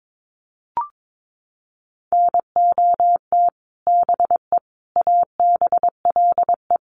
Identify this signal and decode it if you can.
Morse code, heard as a keyed tone, about 22 wpm, 700 Hz. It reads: NOT BE ABLE
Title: Sets of 4 Words 22wpm